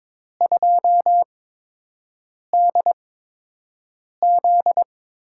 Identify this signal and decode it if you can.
Morse code, heard as a keyed tone, about 22 wpm, spaced as for 6 wpm, 700 Hz. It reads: 2DZ
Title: USA Call Signs - 4 Characters - 8x Character Spacing - 22wpm